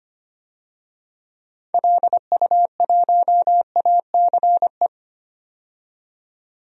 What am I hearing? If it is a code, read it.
Morse code, heard as a keyed tone, about 25 wpm, 700 Hz. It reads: LU1ACE
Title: DXCC-100 Call Signs - 6 Characters - 25wpm